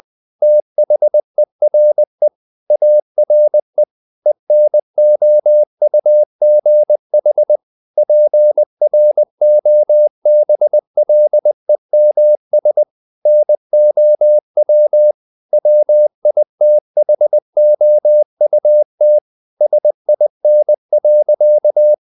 Morse code, heard as a keyed tone, about 20 wpm, 600 Hz. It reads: THERE ARE ENOUGH PROBLEMS NOW WITHOUT SIN.
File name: the old man and the sea part 2 20wpm